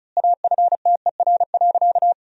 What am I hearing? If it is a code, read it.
Morse code, heard as a keyed tone, about 36 wpm, 700 Hz. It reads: AFTER.
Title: 2x Word-Spacing - Sentences from Top 500 Words - 35wpm